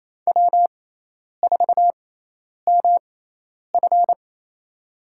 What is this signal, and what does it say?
Morse code, heard as a keyed tone, about 28 wpm, spaced as for 10 wpm, 700 Hz. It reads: W4MF
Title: USA Call Signs - 6x Character Spacing - 28wpm